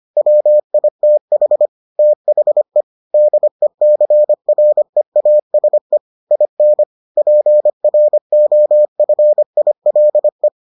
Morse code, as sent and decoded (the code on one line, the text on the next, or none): .-- .. - .... / - .... / -.. -.-. .-. .- ... / .. -. / .--. .-. --- ..-. .. .-.. .
WITH THE DECREASE IN PROFILE